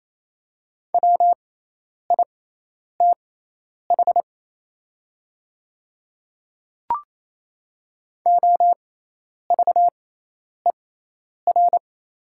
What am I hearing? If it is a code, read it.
Morse code, heard as a keyed tone, about 28 wpm, spaced as for 10 wpm, 700 Hz. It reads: WITH OVER
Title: 6x Character Spacing - Top 100 Words - 28wpm